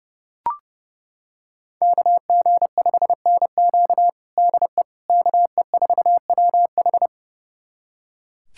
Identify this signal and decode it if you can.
Morse code, heard as a keyed tone, about 30 wpm, 700 Hz. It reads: KG5NQ DE KE4WH
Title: USA Callsign DE Callsign 30wpm